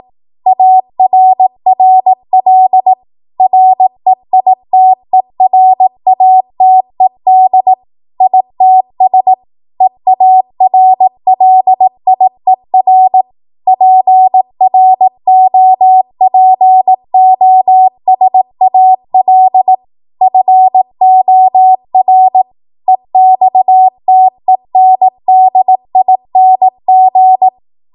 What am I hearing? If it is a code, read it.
Morse code, heard as a keyed tone, about 18 wpm, 750 Hz. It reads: ARRL REITERATED ITS EARLIER PROPOSAL FOR EXTENDING
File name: Morse / 18WPM